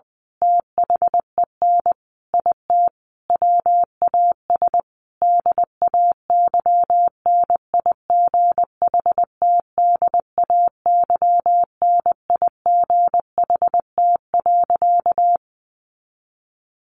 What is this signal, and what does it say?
Morse code, heard as a keyed tone, about 20 wpm, 700 Hz. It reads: THEN IT WAS DAYNIGHTDAYNIGHT.
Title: A Sound Of Thunder 20wpm